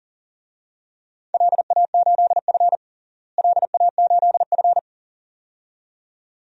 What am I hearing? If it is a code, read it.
Morse code, heard as a keyed tone, about 37 wpm, 700 Hz. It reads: LA8F LA8F
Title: DXCC-100 Call Signs - 4 Characters - Repeated 2x - 40wpm